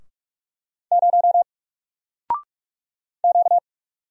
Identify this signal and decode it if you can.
Morse code, heard as a keyed tone, about 46 wpm, 700 Hz. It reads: 0 Q
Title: Single Letter-Number Alt 45wpm